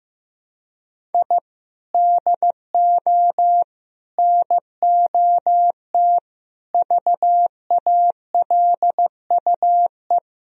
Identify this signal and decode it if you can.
Morse code, heard as a keyed tone, about 15 wpm, 700 Hz. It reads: I DO NOT VALUE
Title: Sentences from Top 1000 Words - Review Part 4 - 15wpm